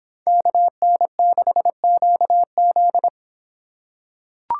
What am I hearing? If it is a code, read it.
Morse code, heard as a keyed tone, about 26 wpm, 700 Hz. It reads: KN6QZ